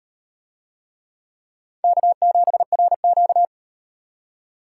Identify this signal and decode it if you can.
Morse code, heard as a keyed tone, about 33 wpm, 700 Hz. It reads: K7RQ